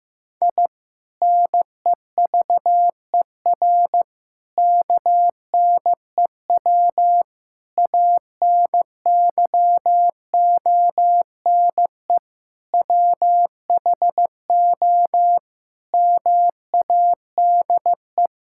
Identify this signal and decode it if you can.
Morse code, heard as a keyed tone, about 15 wpm, 700 Hz. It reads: I NEVER KNEW ANYONE WHO MADE
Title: Sentences from Top 1000 Words - Review Part 6 - 15wpm